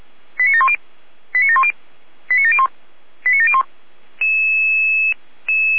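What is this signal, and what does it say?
Five-tone selective call (ZVEI), heard as quick runs of five tones, tones of about 70 ms. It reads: 89710 89710 89701 89701
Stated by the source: ZVEI five-tone selective call received by SDR.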